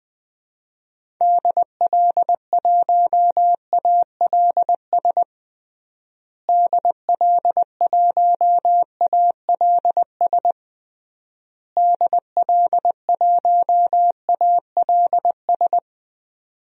Morse code, heard as a keyed tone, about 20 wpm, 700 Hz. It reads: DL1ALS DL1ALS DL1ALS
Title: DXCC-100 Call Signs - 6 Characters - Repeated 3x - 20wpm